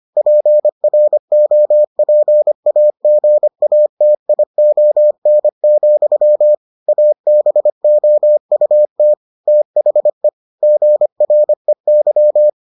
Morse code, heard as a keyed tone, about 25 wpm, 600 Hz. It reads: PROPAGATION, ABOUT THE GREY